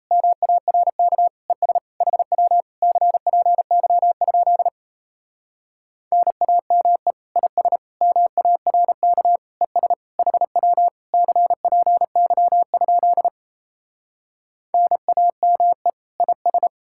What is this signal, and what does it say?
Morse code, heard as a keyed tone, about 37 wpm, 700 Hz. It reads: MARK ES HW CPY? NAME IS MARK ES HW CPY? NAME IS